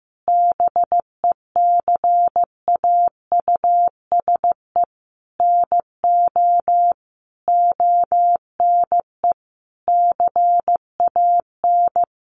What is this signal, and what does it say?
Morse code, heard as a keyed tone, about 14 wpm, 700 Hz. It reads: BECAUSE NO ONE CAN
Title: Sentences from Top 600 Words 15wpm